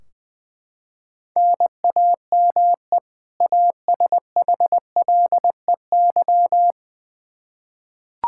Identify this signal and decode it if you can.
Morse code, heard as a keyed tone, about 20 wpm, 700 Hz. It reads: NAME ASHLEY
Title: Practice QSO - Exchange 1 - Beginning - 20wpm